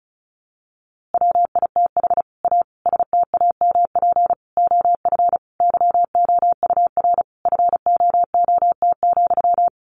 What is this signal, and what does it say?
Morse code, heard as a keyed tone, about 34 wpm, 700 Hz. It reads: WITH A STAMP OF YOUR FOOT,
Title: A Sound Of Thunder 35wpm